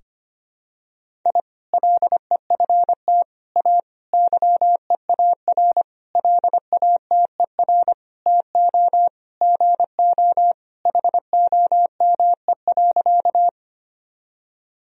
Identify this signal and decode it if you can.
Morse code, heard as a keyed tone, about 25 wpm, 700 Hz. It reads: I LEFT A YEAR LATER TO GO HOME.
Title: Sentences from Top 1000 Words - Review Part 4 - 25wpm